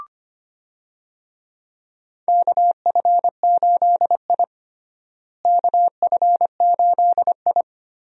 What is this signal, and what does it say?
Morse code, heard as a keyed tone, about 25 wpm, 700 Hz. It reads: KF8I KF8I
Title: USA Call Signs - 4 Characters - Repeated 2x - 25wpm